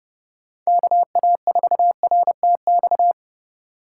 Morse code, heard as a keyed tone, about 33 wpm, 700 Hz. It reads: KA4RTX